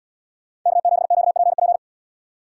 Morse code, heard as a keyed tone, about 77 wpm, 700 Hz. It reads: K7QPP